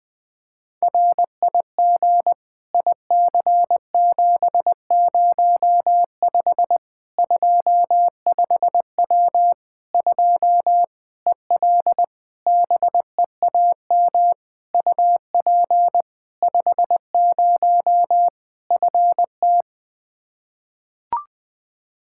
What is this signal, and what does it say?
Morse code, heard as a keyed tone, about 20 wpm, 700 Hz. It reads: RIG IC705 25W 2 EL BEAM UP 50 FT
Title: Practice QSO - Exchange 2 - Info - 20wpm